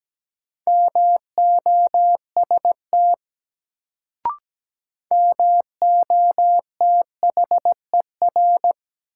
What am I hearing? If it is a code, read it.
Morse code, heard as a keyed tone, about 17 wpm, 700 Hz. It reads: MOST MOTHER
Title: Sentences from Top 200 Words - New Word Drill - 17wpm